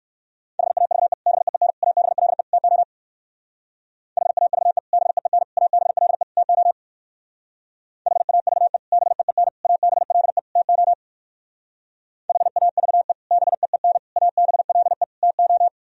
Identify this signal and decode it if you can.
Morse code, heard as a keyed tone, about 60 wpm, 700 Hz. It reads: HAVE BEEN ABLE TO HAVE BEEN ABLE TO HAVE BEEN ABLE TO HAVE BEEN ABLE TO